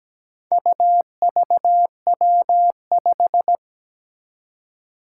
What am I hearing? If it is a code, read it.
Morse code, heard as a keyed tone, about 17 wpm, 700 Hz. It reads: UVW5
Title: Four Challenging Letter-Number Combinations 17wpm